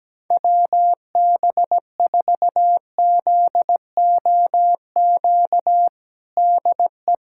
Morse code, heard as a keyed tone, about 17 wpm, 700 Hz. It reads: WB4ZOQ DE
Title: USA Callsign DE Callsign 17wpm